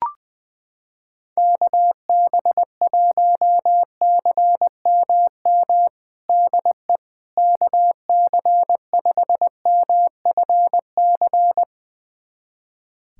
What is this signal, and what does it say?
Morse code, heard as a keyed tone, about 20 wpm, 700 Hz. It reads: KB1CMM DE KC5MFC
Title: USA Callsign DE Callsign - 6 Characters - 20wpm